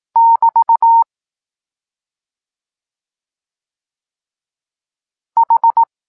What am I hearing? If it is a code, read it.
Morse code, heard as a keyed tone, about 18 wpm, 920 Hz. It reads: <BT> H
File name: Tekst-26 1